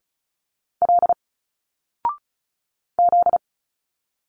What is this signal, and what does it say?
Morse code, heard as a keyed tone, about 34 wpm, 700 Hz. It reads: L Z
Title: Single Letters 35wpm